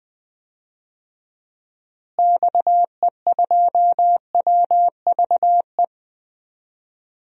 Morse code, heard as a keyed tone, about 20 wpm, 700 Hz. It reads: XE2WVE
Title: DXCC-100 Call Signs - 6 Characters - Repeated 2x - 20wpm